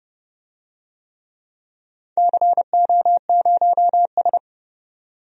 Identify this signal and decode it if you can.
Morse code, heard as a keyed tone, about 32 wpm, 700 Hz. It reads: CO0S
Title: DXCC-100 Call Signs - 4 Characters - 30wpm